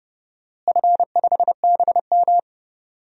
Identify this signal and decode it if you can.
Morse code, heard as a keyed tone, about 30 wpm, 700 Hz. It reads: F5BM
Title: DXCC-100 Call Signs - 4 Characters - Repeated 3x - 30wpm